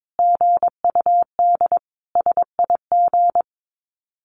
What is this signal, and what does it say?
Morse code, heard as a keyed tone, about 22 wpm, 700 Hz. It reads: GUD SIG
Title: Common QSO Phrases 22wpm